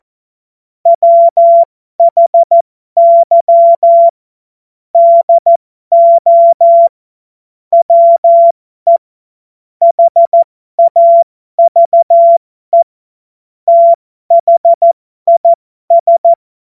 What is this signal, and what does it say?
Morse code, heard as a keyed tone, about 14 wpm, spaced as for 12 wpm, 680 Hz. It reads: WHY DO WE HAVE THIS
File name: Morsekurs / Witze